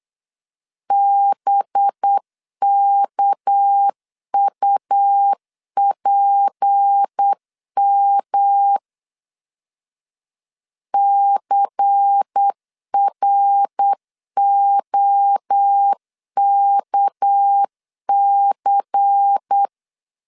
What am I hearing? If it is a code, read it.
Morse code, heard as a keyed tone, about 8 wpm, 800 Hz. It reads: BKUPM CROKC
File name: LEZIONI / 6 PKX0 / 6 PK 8wpm 3003